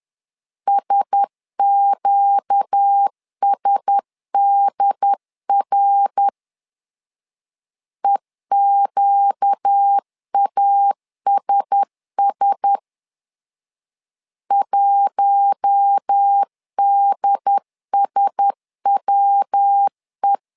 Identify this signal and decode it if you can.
Morse code, heard as a keyed tone, about 10 wpm, 800 Hz. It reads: SQSDR EQASS 1DSWE
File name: LEZIONI / 4 Q1S 10wpm 2133